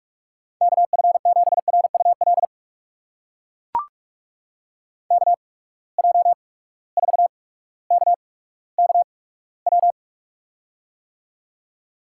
Morse code, heard as a keyed tone, about 44 wpm, spaced as for 19 wpm, 700 Hz. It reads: KF7RUL KJ4KKW
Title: USA Call Signs - 6 Characters - 8x Character Spacing - 45wpm